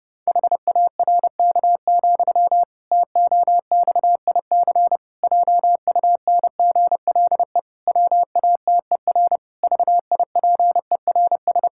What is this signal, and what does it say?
Morse code, heard as a keyed tone, about 30 wpm, 700 Hz. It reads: HARK, TOXIC JUNGLE WATER VIPERS
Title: Pangram Challenge v2 - 30wpm